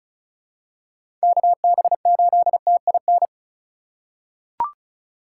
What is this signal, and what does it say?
Morse code, heard as a keyed tone, about 34 wpm, 700 Hz. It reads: KB8TIN
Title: USA Call Signs - 6 Characters - 2x Character Spacing - 35wpm